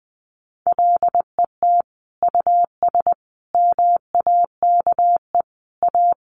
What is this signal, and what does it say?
Morse code, heard as a keyed tone, about 20 wpm, 700 Hz. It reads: LET US MAKE A
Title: Sentences from Top 1500 Words 20wpm